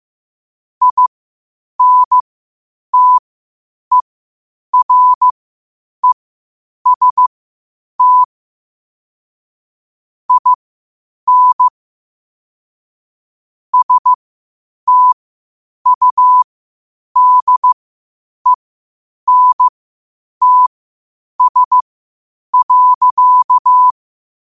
Morse code, heard as a keyed tone, about 15 wpm, spaced as for 8 wpm, 1000 Hz. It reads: INTEREST IN STUDENTS.
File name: qotdfiles / 8wpm